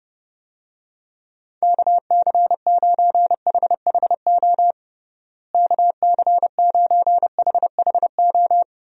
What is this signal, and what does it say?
Morse code, heard as a keyed tone, about 30 wpm, 700 Hz. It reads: KC9HHO KC9HHO
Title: USA Call Signs - 6 Characters - Repeated 2x - 30wpm